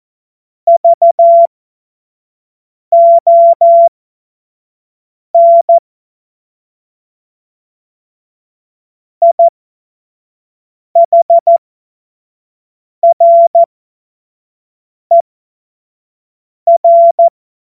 Morse code, heard as a keyed tone, about 14 wpm, spaced as for 5 wpm, 680 Hz. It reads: VON IHRER
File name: Morsekurs / Witze / Witz 06. Dezember 2022 25 F70